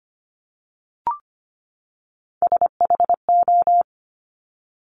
Morse code, heard as a keyed tone, about 25 wpm, 700 Hz. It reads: SHO
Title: Top Three Letter Combinations in Words 25wpm